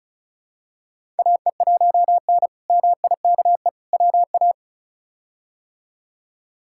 Morse code, heard as a keyed tone, about 36 wpm, 700 Hz. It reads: AE1N MIKE WA
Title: K1USN SST QSO Practice - 35wpm